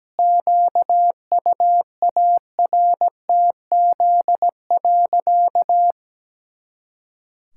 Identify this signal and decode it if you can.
Morse code, heard as a keyed tone, about 17 wpm, 700 Hz. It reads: QUARTZ.
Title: Pangram Challenge v2 - 17wpm